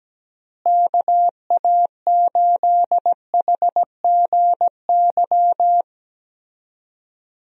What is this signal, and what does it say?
Morse code, heard as a keyed tone, about 17 wpm, 700 Hz. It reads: KA8HGY
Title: USA Call Signs - 6 Characters - Repeated 2x - 17wpm